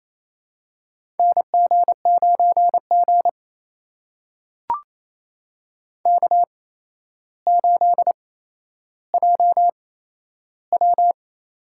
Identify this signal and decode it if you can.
Morse code, heard as a keyed tone, about 28 wpm, spaced as for 12 wpm, 700 Hz. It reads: NG9G K8JW
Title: USA Call Signs - 4 Characters - 8x Character Spacing - 28wpm